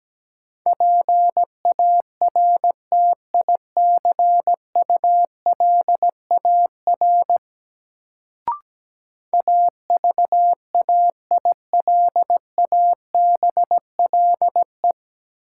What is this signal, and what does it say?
Morse code, heard as a keyed tone, about 17 wpm, 700 Hz. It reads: PARTICULAR AVAILABLE
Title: Sentences from Top 600 Words - New Word Drill - 17wpm